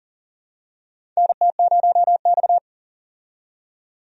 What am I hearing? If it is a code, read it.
Morse code, heard as a keyed tone, about 40 wpm, 700 Hz. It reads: NT0X
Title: USA Call Signs - 4 Characters - 40wpm